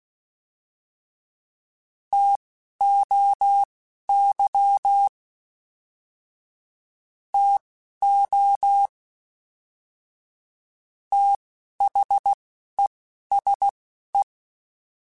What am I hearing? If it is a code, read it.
Morse code, heard as a keyed tone, about 16 wpm, spaced as for 12 wpm, 800 Hz. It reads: TOY TO THESE